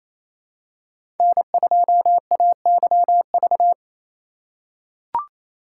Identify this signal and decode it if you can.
Morse code, heard as a keyed tone, about 28 wpm, 690 Hz. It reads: N2AYV
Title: USA Call Signs - 5 Characters - 4x Character Spacing - 28wpm